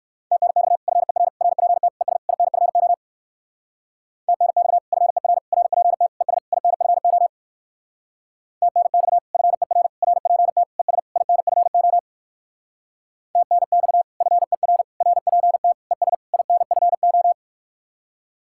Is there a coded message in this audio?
Yes: Morse code from 0.3 to 17.3 s, heard as a keyed tone, about 59 wpm, 700 Hz. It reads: TNX FER RPT ES INFO TNX FER RPT ES INFO TNX FER RPT ES INFO TNX FER RPT ES INFO